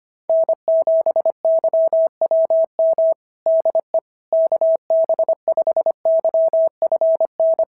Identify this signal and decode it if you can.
Morse code, heard as a keyed tone, about 25 wpm, 650 Hz. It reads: N7YWM DE KB5YFN
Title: SOTA QSO Practice 25wpm